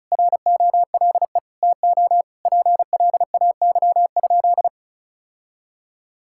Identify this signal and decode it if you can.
Morse code, heard as a keyed tone, about 34 wpm, 700 Hz. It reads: ROLE TO PLAY?
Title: Sentences from Top 1000 Words - Review Part 6 - 35wpm